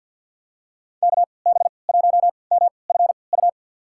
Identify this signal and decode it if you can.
Morse code, heard as a keyed tone, about 46 wpm, spaced as for 27 wpm, 700 Hz. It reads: KB1MFU
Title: USA Call Signs - 6 Characters - 3x Character Spacing - 50wpm